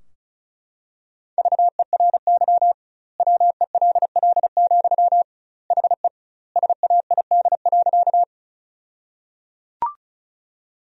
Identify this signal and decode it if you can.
Morse code, heard as a keyed tone, about 36 wpm, 700 Hz. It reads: VERY WELL, HE SAID.
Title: 2x Word-Spacing - Sentences from Top 200 Words - 35wpm